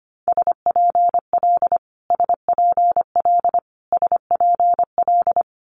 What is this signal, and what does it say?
Morse code, heard as a keyed tone, about 25 wpm, 700 Hz. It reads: SPL SPL SPL
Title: Multi-letter Phonemes Repeated 3x 25wpm